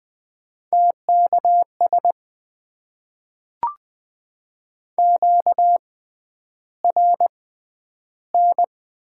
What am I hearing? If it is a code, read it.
Morse code, heard as a keyed tone, about 20 wpm, spaced as for 10 wpm, 700 Hz. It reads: TKS QRN
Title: Top 90 QSO Elements - 6x Character Spacing - 20wpm